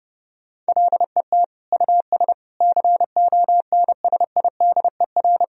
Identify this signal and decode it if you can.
Morse code, heard as a keyed tone, about 32 wpm, 700 Hz. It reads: LET US CONSIDER
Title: Sentences from Top 1000 Words - Review Part 4 - 30wpm